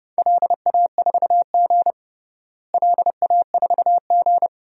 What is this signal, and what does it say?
Morse code, heard as a keyed tone, about 30 wpm, 700 Hz. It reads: LA4G LA4G
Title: DXCC-100 Call Signs - 4 Characters - Repeated 2x - 30wpm